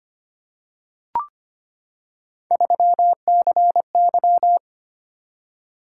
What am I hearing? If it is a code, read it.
Morse code, heard as a keyed tone, about 25 wpm, 700 Hz. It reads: 3CY
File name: Three Letter-Number Combinations - 25wpm